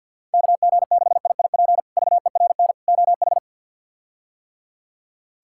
Morse code, heard as a keyed tone, about 53 wpm, 700 Hz. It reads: KG6IIP VERN OH
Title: K1USN SST QSO Practice - 50wpm